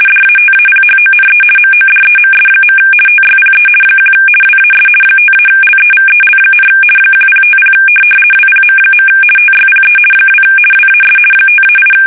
RTTY, about 50 baud. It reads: DATAPHONE MODEMS. TYPE 1 WAS 5-LEVEL, 1050 WPM. TYPE 2 WAS 5-8
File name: fsk850-50B / teletype-corp